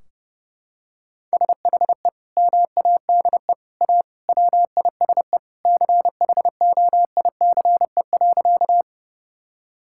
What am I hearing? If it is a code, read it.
Morse code, heard as a keyed tone, about 30 wpm, 700 Hz. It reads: SHE MADE A WISE CHOICE.
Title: Sentences from Top 1600 Words 30wpm